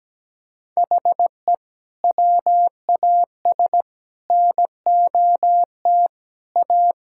Morse code, heard as a keyed tone, about 17 wpm, 700 Hz. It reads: HE WAS NOT A
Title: Sentences from Top 1000 Words - Review Part 5 - 17wpm